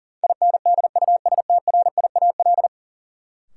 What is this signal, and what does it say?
Morse code, heard as a keyed tone, about 40 wpm, 700 Hz. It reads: INDUSTRIAL
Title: Sets of 5 Words - New Word Drill - 40wpm